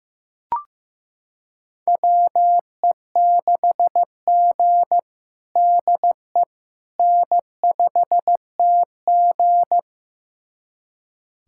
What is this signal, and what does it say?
Morse code, heard as a keyed tone, about 15 wpm, 700 Hz. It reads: WE6G DE N5TG
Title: USA Callsign DE Callsign - 4 Characters - 15wpm